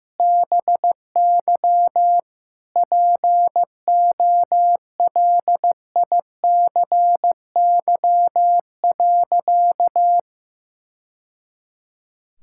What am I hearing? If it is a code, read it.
Morse code, heard as a keyed tone, about 15 wpm, 700 Hz. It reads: BY POLICY.
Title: Sentences from Top 1000 Words - Review Part 5 - 15wpm